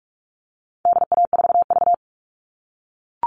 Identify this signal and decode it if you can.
Morse code, heard as a keyed tone, about 44 wpm, 700 Hz. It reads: DA4V
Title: All Cut Numbers - 4-Digit 45wpm